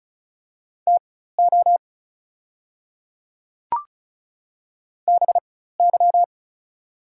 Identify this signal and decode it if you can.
Morse code, heard as a keyed tone, about 36 wpm, spaced as for 16 wpm, 700 Hz. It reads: TO BY
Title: 4x Character Spacing - Top 100 Words - 35wpm